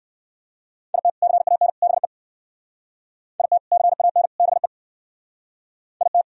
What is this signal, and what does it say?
Morse code, heard as a keyed tone, about 62 wpm, 700 Hz. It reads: IT CAN BE IT CAN BE IT